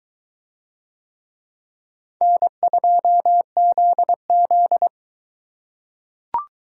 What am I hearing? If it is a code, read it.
Morse code, heard as a keyed tone, about 22 wpm, 700 Hz. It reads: N2ZZ